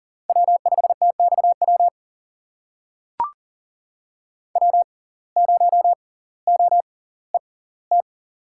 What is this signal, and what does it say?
Morse code, heard as a keyed tone, about 40 wpm, spaced as for 20 wpm, 700 Hz. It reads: W5TXW W0OET
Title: USA Call Signs - 5 Characters - 6x Character Spacing - 40wpm